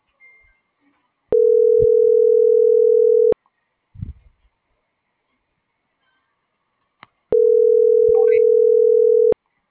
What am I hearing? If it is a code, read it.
Ringback tone (North America), 440 + 480 Hz.